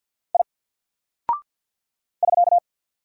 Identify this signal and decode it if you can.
Morse code, heard as a keyed tone, about 48 wpm, 700 Hz. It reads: I <SK>
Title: Single Character (Letter:Number:Punctuation:Prosign) 50wpm